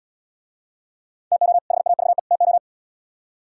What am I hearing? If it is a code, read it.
Morse code, heard as a keyed tone, about 77 wpm, 700 Hz. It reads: TO HAVE TO